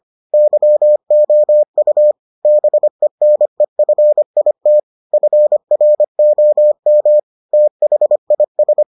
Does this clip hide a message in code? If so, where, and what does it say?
Yes: Morse code from 0.3 to 8.8 s, heard as a keyed tone, about 25 wpm, 600 Hz. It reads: YOU BENEFIT FROM THIS